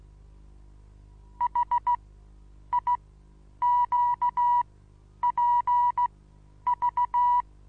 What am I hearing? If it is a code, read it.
Morse code, heard as a keyed tone, about 16 wpm, spaced as for 9 wpm, 970 Hz. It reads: HIQPV
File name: Onlinelehrgang / morsen / DJ4UF Morsen25